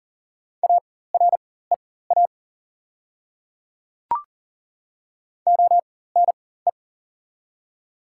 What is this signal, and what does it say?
Morse code, heard as a keyed tone, about 37 wpm, spaced as for 18 wpm, 700 Hz. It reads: AREA ONE
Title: 4x Character Spacing - Top 500 Words - 40wpm